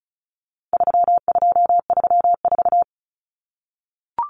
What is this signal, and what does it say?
Morse code, heard as a keyed tone, about 34 wpm, 700 Hz. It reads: 3234
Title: Common Cut Numbers - 4-Digit 35wpm